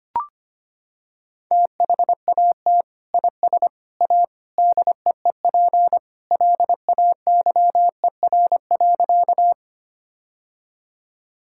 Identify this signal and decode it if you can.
Morse code, heard as a keyed tone, about 25 wpm, 700 Hz. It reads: THAT IS A DEEP LAYER.
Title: Sentences from Top 1600 Words 25wpm